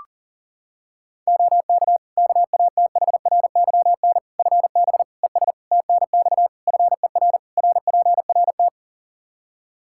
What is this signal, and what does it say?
Morse code, heard as a keyed tone, about 40 wpm, 700 Hz. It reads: OK KATHRYN FB ES TNX FER RPRT
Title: Practice QSO - Exchange 2 - Info - 40wpm